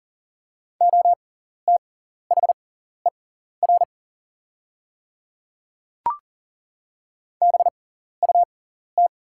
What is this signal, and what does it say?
Morse code, heard as a keyed tone, about 40 wpm, spaced as for 14 wpm, 700 Hz. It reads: OTHER BUT
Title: 6x Character Spacing - Top 100 Words - 40wpm